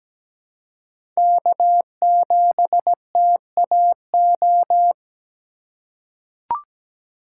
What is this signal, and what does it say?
Morse code, heard as a keyed tone, about 17 wpm, 700 Hz. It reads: K7TAO
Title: USA Call Signs - 5 Characters - 3x Character Spacing - 17wpm